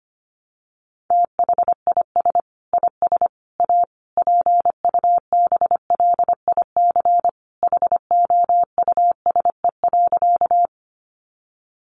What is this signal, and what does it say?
Morse code, heard as a keyed tone, about 25 wpm, 700 Hz. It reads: THIS IS A PUBLIC HOUSE.
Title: Sentences from Top 600 Words 25wpm